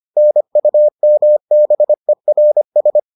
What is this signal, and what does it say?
Morse code, heard as a keyed tone, about 25 wpm, 600 Hz. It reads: NUMBERS